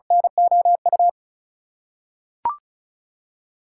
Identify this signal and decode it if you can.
Morse code, heard as a keyed tone, about 34 wpm, 700 Hz. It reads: NOU
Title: USA Call Signs - 6 Characters - 2x Character Spacing - 35wpm